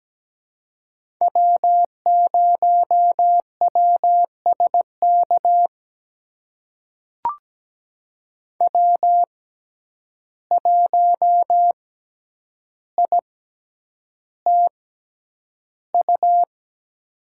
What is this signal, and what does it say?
Morse code, heard as a keyed tone, about 17 wpm, spaced as for 9 wpm, 700 Hz. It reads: W0WSK W1ITU
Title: USA Call Signs - 5 Characters - 6x Character Spacing - 17wpm